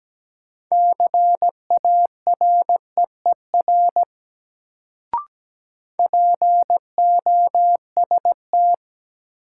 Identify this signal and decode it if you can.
Morse code, heard as a keyed tone, about 17 wpm, 700 Hz. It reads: CAREER POST
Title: Sentences from Top 1000 Words - New Word Drill - 17wpm